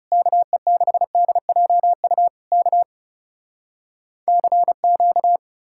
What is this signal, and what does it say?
Morse code, heard as a keyed tone, about 40 wpm, 700 Hz. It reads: KE6DJU K CQ